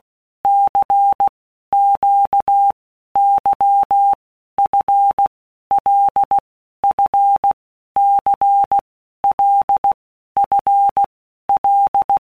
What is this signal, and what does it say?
Morse code, heard as a keyed tone, about 16 wpm, 800 Hz. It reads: C Q Y F L F C L F L
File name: Basic Letter Groups By Blocks / Groups@15 / LFCYQ@15